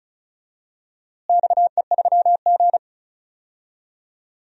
Morse code, heard as a keyed tone, about 34 wpm, 700 Hz. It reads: XE3G
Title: DXCC-100 Call Signs - 4 Characters - Repeated 2x - 35wpm